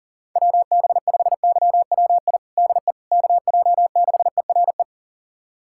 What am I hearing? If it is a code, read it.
Morse code, heard as a keyed tone, about 37 wpm, 700 Hz. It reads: WB5YWI DE KJ6ERE
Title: USA Callsign DE Callsign - 6 Characters - 40wpm